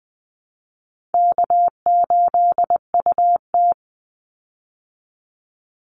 Morse code, heard as a keyed tone, about 20 wpm, 700 Hz. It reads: K8UT
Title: Call Signs Repeated 3x - 20wpm